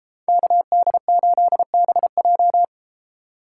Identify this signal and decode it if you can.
Morse code, heard as a keyed tone, about 33 wpm, 700 Hz. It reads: KD8BJ